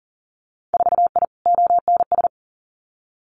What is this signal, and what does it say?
Morse code, heard as a keyed tone, about 38 wpm, 700 Hz. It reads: 4I ONS
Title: Winter Field Day Exchanges Encore 40wpm